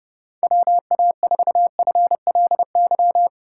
Morse code, heard as a keyed tone, about 30 wpm, 700 Hz. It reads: WA4FLY
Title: USA Call Signs - 6 Characters - 4x Character Spacing - 30wpm